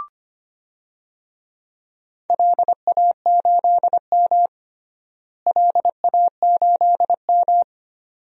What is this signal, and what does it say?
Morse code, heard as a keyed tone, about 25 wpm, 700 Hz. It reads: LA8M LA8M
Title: DXCC-100 Call Signs - 4 Characters - Repeated 2x - 25wpm